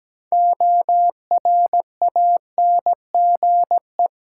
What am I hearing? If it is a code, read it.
Morse code, heard as a keyed tone, about 17 wpm, 700 Hz. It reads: ORANGE
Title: Six Letter Words - 17wpm